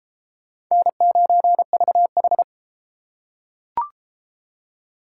Morse code, heard as a keyed tone, about 30 wpm, 700 Hz. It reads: N9VH